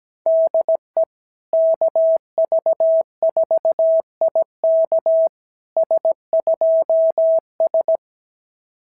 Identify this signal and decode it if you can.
Morse code, heard as a keyed tone, about 17 wpm, 650 Hz. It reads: DE KV4IK S2S
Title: SOTA QSO Practice 17wpm